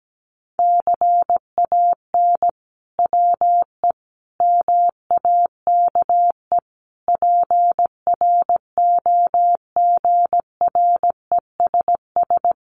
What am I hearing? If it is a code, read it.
Morse code, heard as a keyed tone, about 17 wpm, 700 Hz. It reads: CAN WE MAKE PROGRESS
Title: Sentences from Top 1500 Words 17wpm